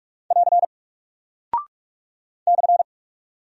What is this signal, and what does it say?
Morse code, heard as a keyed tone, about 44 wpm, 700 Hz. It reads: <AR> /
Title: Single Character (Letter:Number:Punctuation:Prosign) Alt 45wpm